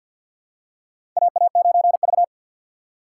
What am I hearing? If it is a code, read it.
Morse code, heard as a keyed tone, about 45 wpm, 700 Hz. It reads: AA9V
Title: USA Call Signs - 4 Characters - 3x Character Spacing - 50wpm